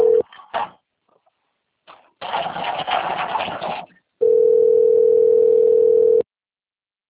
Ringback tone (North America), 440 + 480 Hz.